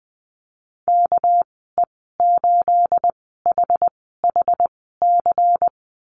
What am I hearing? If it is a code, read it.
Morse code, heard as a keyed tone, about 20 wpm, spaced as for 15 wpm, 700 Hz. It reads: KE8HHC
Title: 238 Lesson 38g - Letter q - Call Signs 20wpm